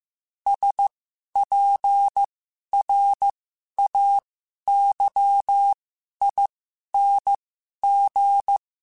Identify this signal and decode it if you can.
Morse code, heard as a keyed tone, about 15 wpm, spaced as for 11 wpm, 800 Hz. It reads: SPRAYING
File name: Special INGs-mp3s / Special INGs-mp3s / Sngl Ing @10